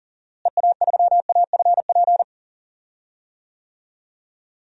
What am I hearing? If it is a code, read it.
Morse code, heard as a keyed tone, about 40 wpm, 700 Hz. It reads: EA3AFP
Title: DXCC-100 Call Signs - 6 Characters - Repeated 2x - 40wpm